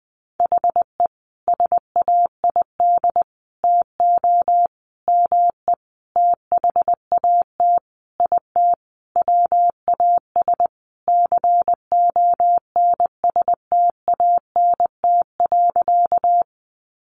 Morse code, heard as a keyed tone, about 20 wpm, 700 Hz. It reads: HE SAID TO ME THAT IT WAS CONSTANT.
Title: Sentences from Top 1500 Words 20wpm